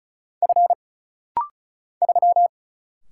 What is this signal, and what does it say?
Morse code, heard as a keyed tone, about 36 wpm, 700 Hz. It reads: F 3
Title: Single Letter-Number Alt - Rapid-Fire 35wpm